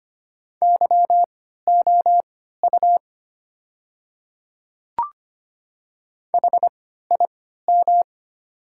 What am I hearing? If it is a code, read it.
Morse code, heard as a keyed tone, about 25 wpm, spaced as for 14 wpm, 700 Hz. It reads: YOU HIM
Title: 3x Character Spacing - Top 100 Words - 25wpm